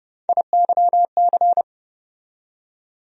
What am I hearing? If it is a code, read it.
Morse code, heard as a keyed tone, about 30 wpm, 700 Hz. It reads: IYC
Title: USA Call Sign Postfixes - 3 Letters 30wpm